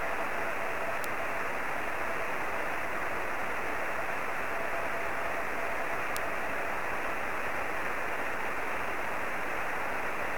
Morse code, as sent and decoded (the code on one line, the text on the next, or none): none